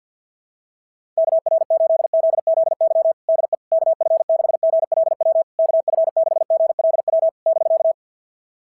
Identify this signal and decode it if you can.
Morse code, heard as a keyed tone, about 52 wpm, 690 Hz. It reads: KR8ZZY DE KF6GLW KF6GLW <BK>
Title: SOTA QSO Practice 50wpm